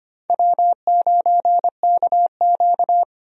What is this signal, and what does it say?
Morse code, heard as a keyed tone, about 25 wpm, 700 Hz. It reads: W9KQ
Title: USA Call Signs - 4 Characters - 25wpm